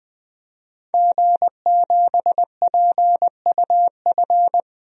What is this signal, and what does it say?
Morse code, heard as a keyed tone, about 20 wpm, 700 Hz. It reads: G7PUF
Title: DXCC-100 Call Signs - 5 Characters - 20wpm